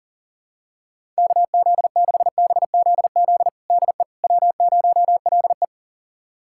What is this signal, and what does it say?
Morse code, heard as a keyed tone, about 40 wpm, 700 Hz. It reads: KZ6BZZ DE W0LE
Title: USA Callsign DE Callsign 40wpm